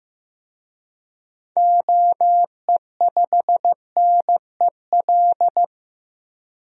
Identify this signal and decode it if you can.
Morse code, heard as a keyed tone, about 15 wpm, 700 Hz. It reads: OE5NEL
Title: DXCC-100 Call Signs - 6 Characters - 15wpm